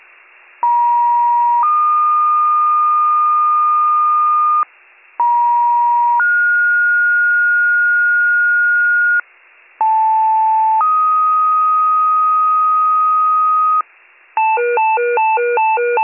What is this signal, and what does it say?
Two-tone page, heard as a one-second tone, then a three-second tone, about 950 then 1220 Hz, about 950 then 1430 Hz, about 880 then 1250 Hz, with noise. It is heard from 0.6 to 13.8 s.